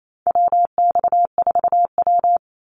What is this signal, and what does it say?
Morse code, heard as a keyed tone, about 28 wpm, 700 Hz. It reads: WX4W
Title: Call Signs - 28wpm